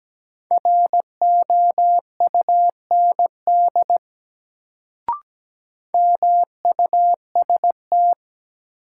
Morse code, heard as a keyed tone, about 17 wpm, 700 Hz. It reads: ROUND MUST
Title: Sentences from Top 200 Words - New Word Drill - 17wpm